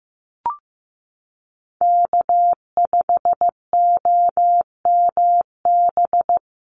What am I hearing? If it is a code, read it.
Morse code, heard as a keyed tone, about 15 wpm, 700 Hz. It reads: K5OMB
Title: USA Call Signs Repeated 2x 15wpm